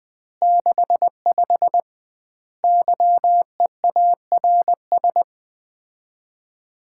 Morse code, heard as a keyed tone, about 21 wpm, 700 Hz. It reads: 65 YEARS
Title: Common QSO Phrases - 2x Word-Spacing - 20wpm